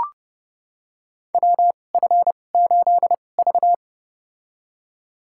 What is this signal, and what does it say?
Morse code, heard as a keyed tone, about 28 wpm, spaced as for 21 wpm, 700 Hz. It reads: WF8V
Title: USA Call Signs - 4 Characters - 2x Character Spacing - 30wpm